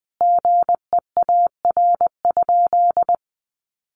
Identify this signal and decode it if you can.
Morse code, heard as a keyed tone, about 20 wpm, 700 Hz. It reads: GEAR?
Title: Sentences from Top 1400 Words 20wpm